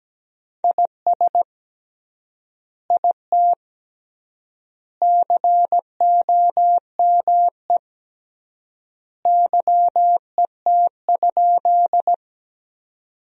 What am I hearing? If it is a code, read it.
Morse code, heard as a keyed tone, about 17 wpm, 700 Hz. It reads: IS IT COME YET?
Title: 3x Word-Spacing - Sentences from Top 500 Words - 17wpm